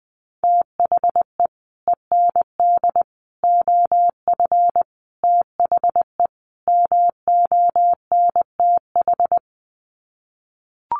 Morse code, heard as a keyed tone, about 20 wpm, 700 Hz. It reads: THE END OF THE MONTH
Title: Sets of 5 Words Encore 20wpm